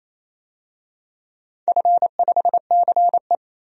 Morse code, heard as a keyed tone, about 28 wpm, 700 Hz. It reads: F5CE
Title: DXCC-100 Call Signs - 4 Characters - 28wpm